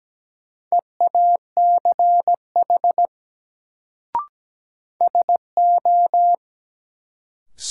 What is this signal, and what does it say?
Morse code, heard as a keyed tone, about 17 wpm, 700 Hz. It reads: EACH SO
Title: Sets of 2 Words - New Word Drill - 17wpm